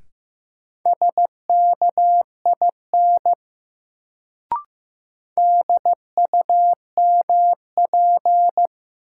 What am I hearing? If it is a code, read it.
Morse code, heard as a keyed tone, about 15 wpm, 700 Hz. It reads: SKIN DUMP
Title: Sentences from Top 1300 Words - New Word Drill Alt - 15wpm